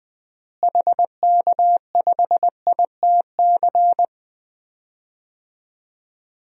Morse code, heard as a keyed tone, about 21 wpm, 700 Hz. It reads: HK5ITC
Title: DXCC-100 Call Signs - 6 Characters - Repeated 2x - 20wpm